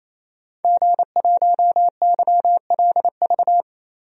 Morse code, heard as a keyed tone, about 27 wpm, 700 Hz. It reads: G1YLV
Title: DXCC-100 Call Signs - 5 Characters - Repeated 3x - 28wpm